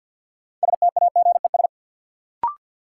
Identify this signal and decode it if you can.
Morse code, heard as a keyed tone, about 48 wpm, 700 Hz. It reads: STAGES
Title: Sets of 5 Words Encore - New Word Drill - 50wpm